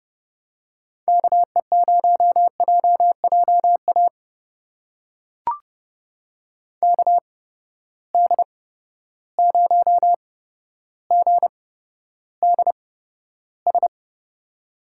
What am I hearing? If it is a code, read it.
Morse code, heard as a keyed tone, about 30 wpm, spaced as for 13 wpm, 700 Hz. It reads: KE0JJA KD0GDS
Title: USA Call Signs - 6 Characters - 8x Character Spacing - 30wpm